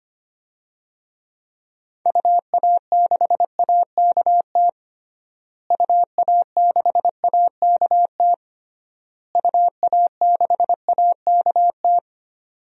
Morse code, heard as a keyed tone, about 25 wpm, 690 Hz. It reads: UA6AKT UA6AKT UA6AKT
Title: DXCC-100 Call Signs - 6 Characters - Repeated 3x - 25wpm